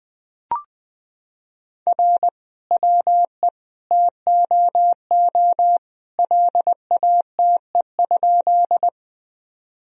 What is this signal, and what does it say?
Morse code, heard as a keyed tone, about 20 wpm, 700 Hz. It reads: R WE TOO LATE?
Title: Sentences from Top 300 Words and Abbreviations 20wpm